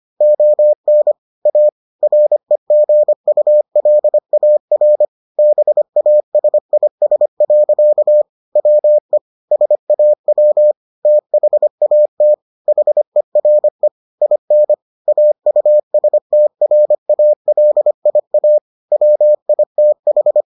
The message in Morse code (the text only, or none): ON A REGULAR BASIS. WE SAW THAT HERE IN AUSTRALIA WITH